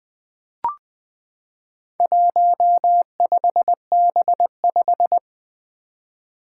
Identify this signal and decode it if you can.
Morse code, heard as a keyed tone, about 20 wpm, 700 Hz. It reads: 15B5
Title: Four Challenging Letter-Number Combinations 20wpm